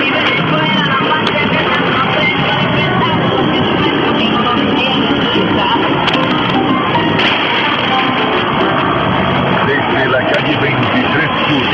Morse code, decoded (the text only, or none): none